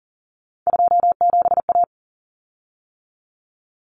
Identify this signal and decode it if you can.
Morse code, heard as a keyed tone, about 40 wpm, 700 Hz. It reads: 27A
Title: Common Cut Numbers - 3-Digit 40wpm